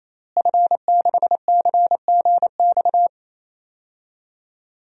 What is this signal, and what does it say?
Morse code, heard as a keyed tone, about 28 wpm, 700 Hz. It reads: F6CGX
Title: DXCC-100 Call Signs - 5 Characters - Repeated 3x - 28wpm